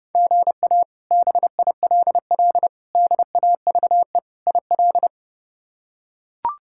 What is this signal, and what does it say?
Morse code, heard as a keyed tone, about 30 wpm, 700 Hz. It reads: GA BILL DAVE IL
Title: K1USN SST QSO Practice - 30wpm